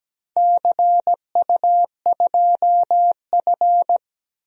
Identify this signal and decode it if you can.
Morse code, heard as a keyed tone, about 17 wpm, 700 Hz. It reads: CU2F
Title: DXCC-100 Call Signs - 4 Characters - 17wpm